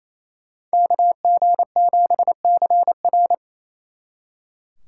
Morse code, heard as a keyed tone, about 28 wpm, 700 Hz. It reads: KG7CR
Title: Speed Racing - USA 2x2 Call Signs 28wpm